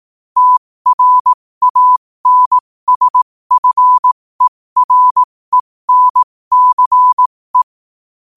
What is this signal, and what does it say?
Morse code, heard as a keyed tone, about 18 wpm, spaced as for 15 wpm, 1000 Hz. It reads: TRANSFERENCE
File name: qotdfiles / 15wpm